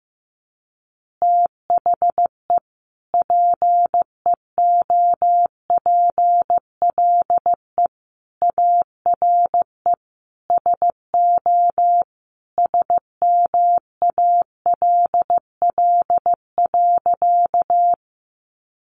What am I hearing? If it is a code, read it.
Morse code, heard as a keyed tone, about 14 wpm, 700 Hz. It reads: THE PEOPLE ARE SO SMALL.
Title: Additional Easy Phrases 15wpm